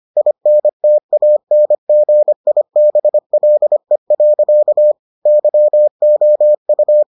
Morse code, heard as a keyed tone, about 25 wpm, 600 Hz. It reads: INTANGIBLE. YOU